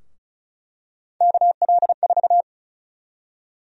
Morse code, heard as a keyed tone, about 36 wpm, 700 Hz. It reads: KL4
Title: USA Call Sign Prefixes - Repeated 2x - 35wpm